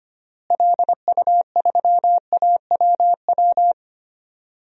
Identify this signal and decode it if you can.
Morse code, heard as a keyed tone, about 25 wpm, 700 Hz. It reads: LU3AWW
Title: DXCC-100 Call Signs - 6 Characters - Repeated 2x - 25wpm